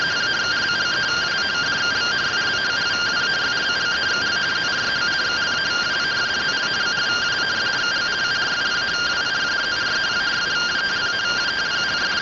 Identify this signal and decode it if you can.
Two-tone teleprinter signal, about 170 Hz shift.